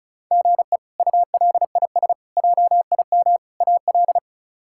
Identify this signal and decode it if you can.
Morse code, heard as a keyed tone, about 34 wpm, 700 Hz. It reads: GE ULIS JIM AL
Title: K1USN SST QSO Practice - 35wpm